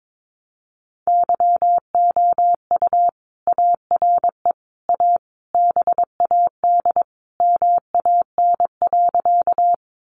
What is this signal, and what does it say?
Morse code, heard as a keyed tone, about 22 wpm, 700 Hz. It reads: YOU ARE A BAD MAN.
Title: Sentences from Top 700 Words 22wpm